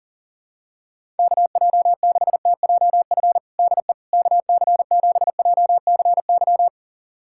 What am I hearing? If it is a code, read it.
Morse code, heard as a keyed tone, about 40 wpm, 700 Hz. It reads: KJ6TJF DE KC7JCY
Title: USA Callsign DE Callsign - 6 Characters - 40wpm